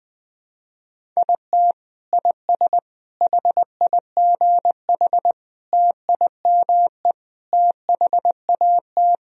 Morse code, heard as a keyed tone, about 20 wpm, 700 Hz. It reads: IT IS HIGH TIME THAT
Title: Sentences from Top 1000 Words - Review Part 1 - 20wpm